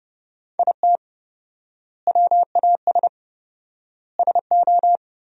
Morse code, heard as a keyed tone, about 30 wpm, 700 Hz. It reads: IT WAS SO
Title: 4x Word-Spacing - Sentences from Top 500 Words - 30wpm